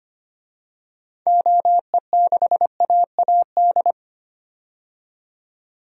Morse code, heard as a keyed tone, about 25 wpm, 700 Hz. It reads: OE6AAD
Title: DXCC-100 Call Signs - 6 Characters - 25wpm